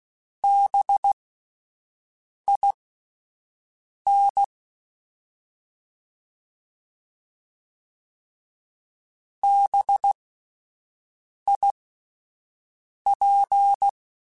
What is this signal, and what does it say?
Morse code, heard as a keyed tone, about 16 wpm, spaced as for 5 wpm, 800 Hz. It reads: BIN BIP
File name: Spaced Alphas / Spaced Alphas mp3s / Bs sp @10